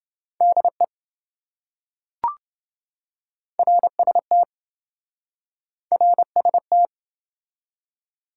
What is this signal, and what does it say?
Morse code, heard as a keyed tone, about 26 wpm, 700 Hz. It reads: DE RST RST